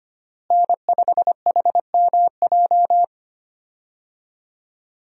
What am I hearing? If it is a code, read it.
Morse code, heard as a keyed tone, about 25 wpm, 700 Hz. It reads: N5HMJ
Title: USA Call Signs - 5 Characters - Repeated 3x - 25wpm